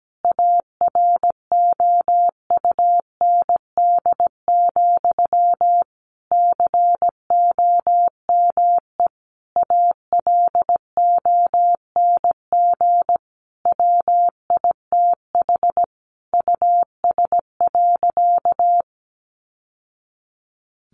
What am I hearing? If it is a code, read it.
Morse code, heard as a keyed tone, about 17 wpm, 700 Hz. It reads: AROUND, COME ALONG WITH US.
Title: Sentences from Top 1200 Words 17wpm